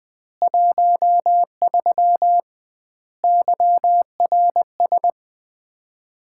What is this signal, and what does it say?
Morse code, heard as a keyed tone, about 20 wpm, 700 Hz. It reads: 13 YRS
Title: Common QSO Phrases - 2x Word-Spacing - 20wpm